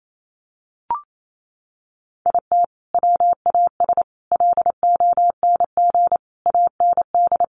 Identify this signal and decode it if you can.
Morse code, heard as a keyed tone, about 28 wpm, 700 Hz. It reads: IT WAS LONG AND
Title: Sentences from Top 1400 Words 28wpm